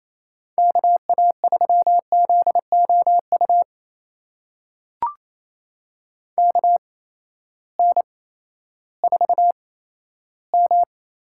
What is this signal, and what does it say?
Morse code, heard as a keyed tone, about 27 wpm, spaced as for 14 wpm, 700 Hz. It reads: KA3ZOU KN4M
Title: USA Call Signs - 6 Characters - 8x Character Spacing - 28wpm